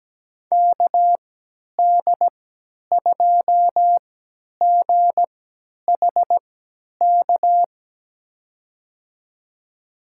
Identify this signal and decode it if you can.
Morse code, heard as a keyed tone, about 17 wpm, spaced as for 10 wpm, 700 Hz. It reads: KD2GHK
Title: USA Call Signs - 3x Character Spacing - 17wpm